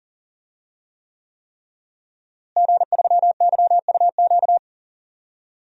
Morse code, heard as a keyed tone, about 38 wpm, 700 Hz. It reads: G3YUQ
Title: DXCC-100 Call Signs - 5 Characters - Repeated 3x - 40wpm